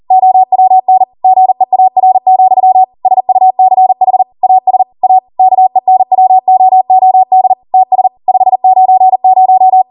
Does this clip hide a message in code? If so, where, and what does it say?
Yes: Morse code from 0.1 to 9.8 s, heard as a keyed tone, about 40 wpm, 750 Hz. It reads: OWN GEAR, SUCH AS A KENWOOD TS 590